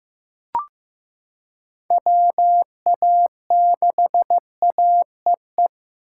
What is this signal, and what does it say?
Morse code, heard as a keyed tone, about 16 wpm, 700 Hz. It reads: WA6AEE
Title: K1USN SST QSO Practice - 15wpm